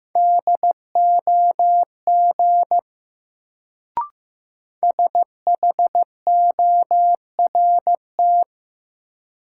Morse code, heard as a keyed tone, about 15 wpm, 700 Hz. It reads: DOG SHORT
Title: Sentences from Top 400 Words - New Word Drill - 15wpm